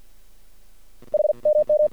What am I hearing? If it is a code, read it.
Morse code, heard as a keyed tone, about 50 wpm, spaced as for 43 wpm, 600 Hz. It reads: HMM
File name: archived / CRSnet